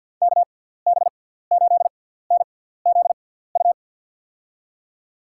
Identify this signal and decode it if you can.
Morse code, heard as a keyed tone, about 53 wpm, spaced as for 17 wpm, 700 Hz. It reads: KB8NZU
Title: USA Call Signs - 6 Characters - 6x Character Spacing - 50wpm